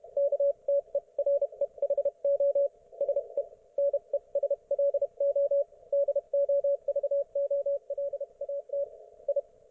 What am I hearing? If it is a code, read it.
Morse code, heard as a keyed tone, about 31 wpm, spaced as for 27 wpm, 560 Hz, with radio noise. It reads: KTEREHO SE NESLO DOVOLAT I